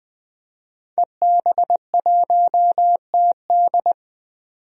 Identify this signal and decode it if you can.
Morse code, heard as a keyed tone, about 20 wpm, 700 Hz. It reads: EB1TD
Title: DXCC-100 Call Signs - 5 Characters - 20wpm